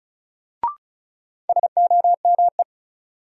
Morse code, heard as a keyed tone, about 36 wpm, 700 Hz. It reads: SOME
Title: Sets of 2 Words - New Word Drill - 35wpm